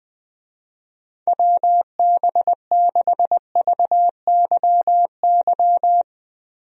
Morse code, heard as a keyed tone, about 20 wpm, 700 Hz. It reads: WB6VYY
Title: USA Call Signs - 6 Characters - Repeated 2x - 20wpm